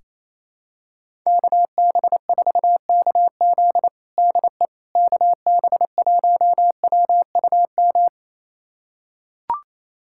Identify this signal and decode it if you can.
Morse code, heard as a keyed tone, about 28 wpm, 700 Hz. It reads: KB4KZ DE KB1WUM
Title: USA Callsign DE Callsign 28wpm